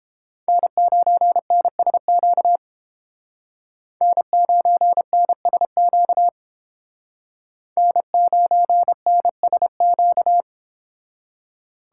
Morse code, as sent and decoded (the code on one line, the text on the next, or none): -. ----. -. ... --.- / -. ----. -. ... --.- / -. ----. -. ... --.-
N9NSQ N9NSQ N9NSQ